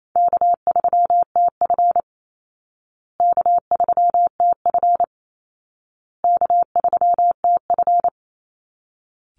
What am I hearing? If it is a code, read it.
Morse code, heard as a keyed tone, about 28 wpm, 700 Hz. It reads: K3TF K3TF K3TF
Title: Call Signs Repeated 2x - 28wpm